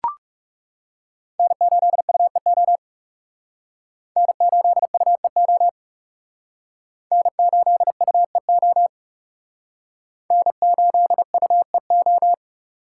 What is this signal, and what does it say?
Morse code, heard as a keyed tone, about 37 wpm, 700 Hz. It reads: N8UEO N8UEO N8UEO N8UEO
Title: Speed Racing - USA 1x3 Call Signs 30wpm